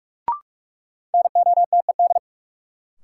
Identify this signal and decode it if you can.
Morse code, heard as a keyed tone, about 44 wpm, 700 Hz. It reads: NOTED
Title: Sets of 5 Words - New Word Drill - 45wpm